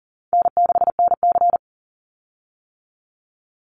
Morse code, heard as a keyed tone, about 40 wpm, 700 Hz. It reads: N6NC
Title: Easy Call Signs - 40wpm